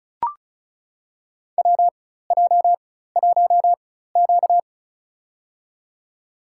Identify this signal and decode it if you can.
Morse code, heard as a keyed tone, about 34 wpm, spaced as for 16 wpm, 700 Hz. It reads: WJ1Q
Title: USA Call Signs - 4x Character Spacing - 35wpm